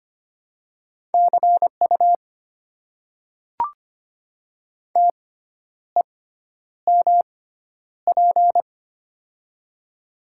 Morse code, heard as a keyed tone, about 25 wpm, spaced as for 10 wpm, 690 Hz. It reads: CU TEMP
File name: Top 90 QSO Elements - 6x Character Spacing - 25wpm